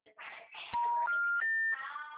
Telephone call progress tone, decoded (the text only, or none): special information tone